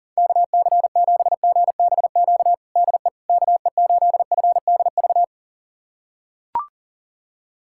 Morse code, heard as a keyed tone, about 40 wpm, 690 Hz. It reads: KC7GBQ DE KE8FDV
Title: USA Callsign DE Callsign - 6 Characters - 40wpm